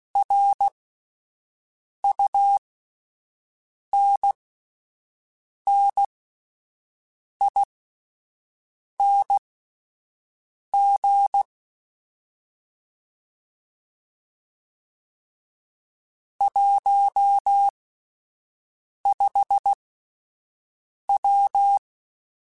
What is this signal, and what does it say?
Morse code, heard as a keyed tone, about 16 wpm, spaced as for 5 wpm, 800 Hz. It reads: RUNNING 15W
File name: QSOs / QSOs-mp3s / WD0BGP 1@13